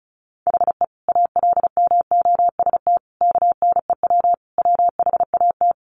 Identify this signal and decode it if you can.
Morse code, heard as a keyed tone, about 34 wpm, 700 Hz. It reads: HE ALMOST KNEW WHAT
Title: Sentences from Top 700 Words 35wpm